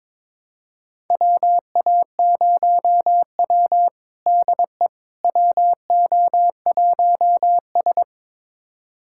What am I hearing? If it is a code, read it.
Morse code, heard as a keyed tone, about 22 wpm, 700 Hz. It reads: WA0W DE WO1S
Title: USA Callsign DE Callsign - 4 Characters - 22wpm